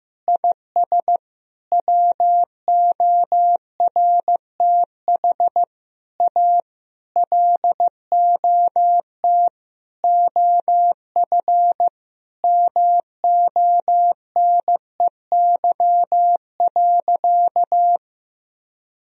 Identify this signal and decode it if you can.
Morse code, heard as a keyed tone, about 15 wpm, 700 Hz. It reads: IS WORTH A LOT OF MONEY.
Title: Sentences from Top 1000 Words - Review Part 4 - 15wpm